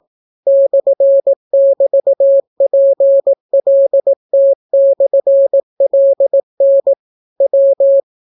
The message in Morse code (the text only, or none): /<BT>PLT/LN W